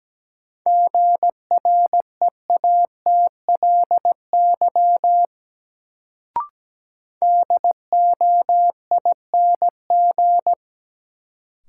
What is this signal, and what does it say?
Morse code, heard as a keyed tone, about 17 wpm, 700 Hz. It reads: GREATLY DOING
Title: Sentences from Top 1400 Words - New Word Drill - 17wpm